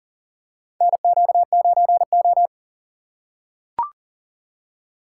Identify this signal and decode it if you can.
Morse code, heard as a keyed tone, about 37 wpm, 700 Hz. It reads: NQ9O
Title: USA Call Signs - 4 Characters - 2x Character Spacing - 40wpm